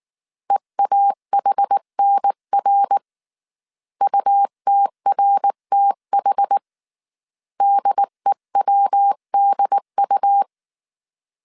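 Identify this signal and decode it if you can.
Morse code, heard as a keyed tone, about 19 wpm, 800 Hz. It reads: EAHNR UTRTH DEWDU
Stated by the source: Morse code dictation at 18 wpm from the IW3SGT QRS Morse code training course, five-character groups of 300 characters.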